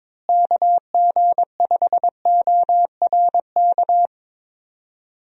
Morse code, heard as a keyed tone, about 22 wpm, 700 Hz. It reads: KG5ORK
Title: USA Call Signs - 6 Characters - Repeated 2x - 22wpm